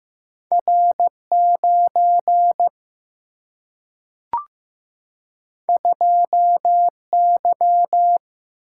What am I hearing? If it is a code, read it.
Morse code, heard as a keyed tone, about 15 wpm, 700 Hz. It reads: R 9 2 Y
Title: Single-Letter Single-Number 15wpm